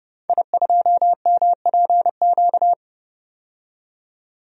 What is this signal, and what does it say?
Morse code, heard as a keyed tone, about 30 wpm, 700 Hz. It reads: I2MPQ
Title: DXCC-100 Call Signs - 5 Characters - 30wpm